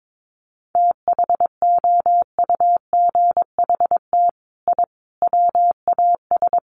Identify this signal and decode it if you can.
Morse code, heard as a keyed tone, about 22 wpm, 700 Hz. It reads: THOUGHT I WAS
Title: Sets of 5 Words Encore2 22wpm